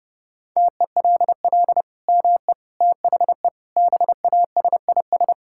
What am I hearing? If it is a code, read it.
Morse code, heard as a keyed tone, about 30 wpm, 700 Hz. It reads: TELL ME THE BASIS
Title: Sentences from Top 1000 Words - Review Part 5 - 30wpm